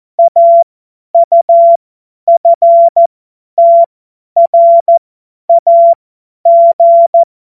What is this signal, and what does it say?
Morse code, heard as a keyed tone, about 14 wpm, spaced as for 10 wpm, 680 Hz. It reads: AUFTRAG
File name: Morsekurs / Witze / Witz 21. Juni 2022 50 F70